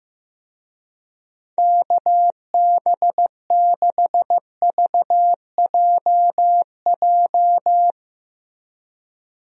Morse code, heard as a keyed tone, about 15 wpm, 700 Hz. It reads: KB6VJJ
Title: USA Call Signs - 6 Characters - Repeated 3x - 15wpm